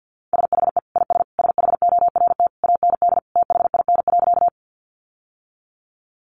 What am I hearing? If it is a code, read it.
Morse code, heard as a keyed tone, about 50 wpm, 700 Hz. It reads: SHE IS SHORT AND THIN.
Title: Sentences from Top 1200 Words 50wpm